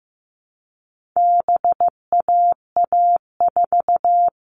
Morse code, heard as a keyed tone, about 15 wpm, 700 Hz. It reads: BAA4
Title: All Cut Numbers - 4-Digit 15wpm